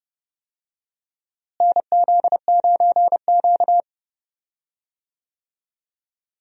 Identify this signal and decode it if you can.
Morse code, heard as a keyed tone, about 31 wpm, 700 Hz. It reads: NZ9Q
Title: USA Call Signs - 4 Characters - Repeated 3x - 30wpm